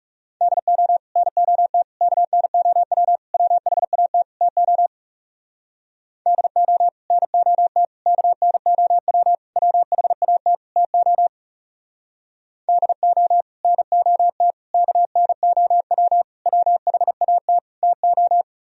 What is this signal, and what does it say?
Morse code, heard as a keyed tone, about 37 wpm, 700 Hz. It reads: DO NOT KNOW WHAT TO DO NOT KNOW WHAT TO DO NOT KNOW WHAT TO
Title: Speed-Racing - Sets of 5 Words 30wpm